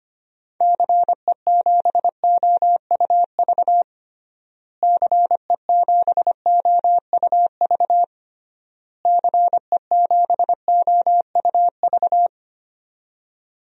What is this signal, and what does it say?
Morse code, heard as a keyed tone, about 25 wpm, 700 Hz. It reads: CE7OUV CE7OUV CE7OUV
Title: DXCC-100 Call Signs - 6 Characters - Repeated 3x - 25wpm